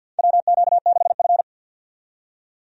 Morse code, heard as a keyed tone, about 48 wpm, 700 Hz. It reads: WQ6F
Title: USA Call Signs - 4 Characters - 50wpm